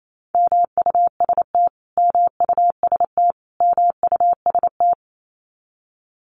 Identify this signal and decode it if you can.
Morse code, heard as a keyed tone, about 28 wpm, 700 Hz. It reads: MUST MUST MUST
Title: Top 300 Words Repeated 28wpm